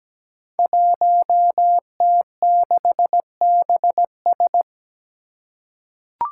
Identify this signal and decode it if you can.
Morse code, heard as a keyed tone, about 17 wpm, 700 Hz. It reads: 1T6BS
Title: Five Challenging Letter-Number Combinations 17wpm